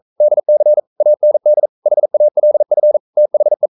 Morse code, heard as a keyed tone, about 42 wpm, 600 Hz. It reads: DC AND HALF THE
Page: Romanen Never af Ken Follet på engelsk.